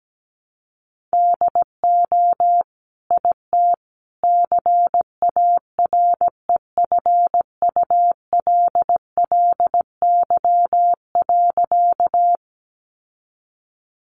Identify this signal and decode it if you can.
Morse code, heard as a keyed tone, about 17 wpm, 700 Hz. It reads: DO IT CAREFULLY.
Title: Sentences from Top 1300 Words 17wpm